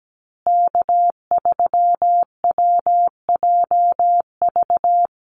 Morse code, heard as a keyed tone, about 17 wpm, 700 Hz. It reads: K3WJV
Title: Call Signs Repeated 3x - 17wpm